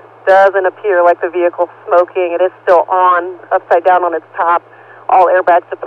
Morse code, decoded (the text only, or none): none